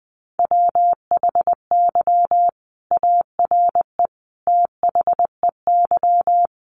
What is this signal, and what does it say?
Morse code, heard as a keyed tone, about 20 wpm, 700 Hz. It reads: WHY ARE THEY
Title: Sentences from Top 1200 Words 20wpm